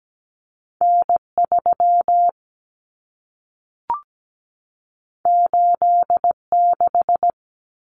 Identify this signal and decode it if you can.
Morse code, heard as a keyed tone, about 17 wpm, 700 Hz. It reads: N3 86
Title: Common Cut Numbers - 2-Digit 17wpm